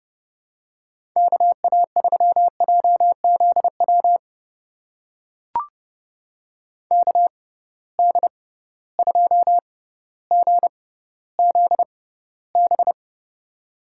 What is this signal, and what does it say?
Morse code, heard as a keyed tone, about 32 wpm, spaced as for 16 wpm, 700 Hz. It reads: KA3JZW KD2GZB
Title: USA Call Signs - 6 Characters - 6x Character Spacing - 30wpm